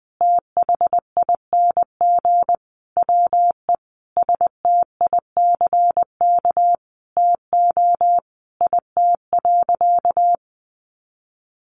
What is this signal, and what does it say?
Morse code, heard as a keyed tone, about 20 wpm, 700 Hz. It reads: THING WE STICK TO IT.
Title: Sentences from Top 1200 Words 20wpm